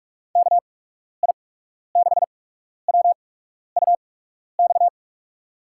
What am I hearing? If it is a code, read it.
Morse code, heard as a keyed tone, about 44 wpm, spaced as for 12 wpm, 700 Hz. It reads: KI6WUX
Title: USA Call Signs - 6 Characters - 8x Character Spacing - 45wpm